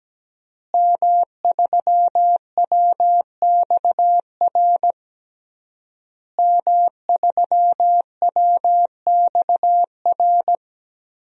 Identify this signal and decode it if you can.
Morse code, heard as a keyed tone, about 17 wpm, 700 Hz. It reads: M3WXR M3WXR
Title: DXCC-100 Call Signs - 5 Characters - Repeated 3x - 17wpm